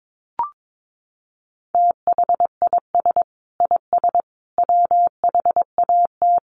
Morse code, heard as a keyed tone, about 22 wpm, 700 Hz. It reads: THIS IS WHAT
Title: Sets of 3 Words 22wpm